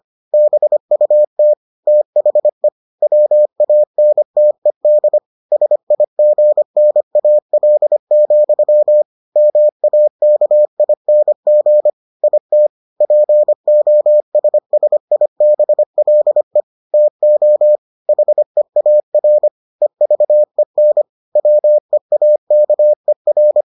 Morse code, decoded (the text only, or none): BUT THE WANTED SIGNAL, MAKING IT POSSIBLE TO HEAR EVEN WEAKER